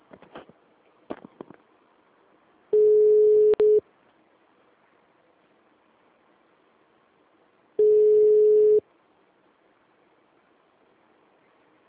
Ringback tone (Europe, 425 Hz).
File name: out- -2015- - - .33